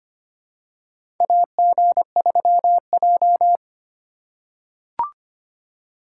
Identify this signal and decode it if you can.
Morse code, heard as a keyed tone, about 25 wpm, 700 Hz. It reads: AG3J
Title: USA Call Signs - 4 Characters - 2x Character Spacing - 25wpm